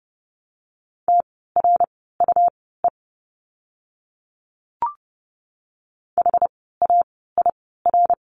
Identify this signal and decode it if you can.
Morse code, heard as a keyed tone, about 30 wpm, spaced as for 17 wpm, 700 Hz. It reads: TRUE HAIR
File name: Lesson 17f - Letter u - Words 30wpm